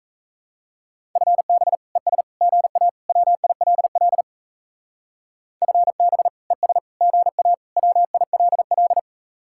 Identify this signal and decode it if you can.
Morse code, heard as a keyed tone, about 34 wpm, 700 Hz. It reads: FB ES GA WILL FB ES GA WILL
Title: Speed-Racing Common QSO Phrases Encore - 28wpm